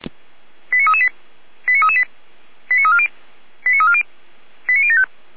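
Five-tone selective call (ZVEI), heard as quick runs of five tones, tones of about 70 ms. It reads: 89338 89338 89340 89340 89075
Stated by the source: ZVEI five-tone selective call received by SDR.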